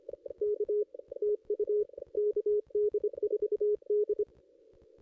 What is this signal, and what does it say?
Morse code, heard as a keyed tone, about 25 wpm, 410 Hz. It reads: K TU KD4D
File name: contests / ss-cw-k5tr-op-n5rz -2200